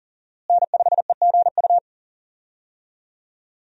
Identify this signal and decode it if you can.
Morse code, heard as a keyed tone, about 40 wpm, 700 Hz. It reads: N5EGU
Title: USA Call Signs - 5 Characters - Repeated 3x - 40wpm